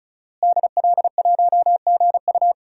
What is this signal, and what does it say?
Morse code, heard as a keyed tone, about 35 wpm, 700 Hz. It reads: DL1GU
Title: DXCC-100 Call Signs - 5 Characters - Repeated 2x - 35wpm